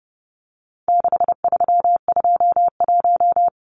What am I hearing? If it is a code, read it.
Morse code, heard as a keyed tone, about 30 wpm, 700 Hz. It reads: 6321
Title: Numbers - 4-Digit 30wpm